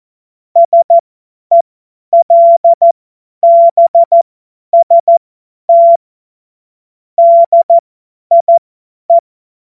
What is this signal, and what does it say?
Morse code, heard as a keyed tone, about 14 wpm, spaced as for 10 wpm, 680 Hz. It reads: SELBST DIE
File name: Morsekurs / Witze / Witz 02. August 2022 50 F70